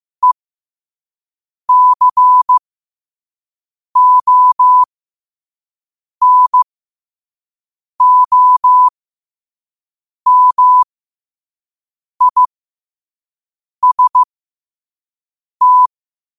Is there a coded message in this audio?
Yes: Morse code from 0.2 to 15.9 s, heard as a keyed tone, about 15 wpm, spaced as for 5 wpm, 1000 Hz. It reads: ECONOMIST